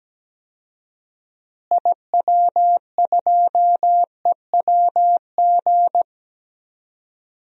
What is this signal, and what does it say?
Morse code, heard as a keyed tone, about 17 wpm, 700 Hz. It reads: IW2EWG
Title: DXCC-100 Call Signs - 6 Characters - Repeated 3x - 17wpm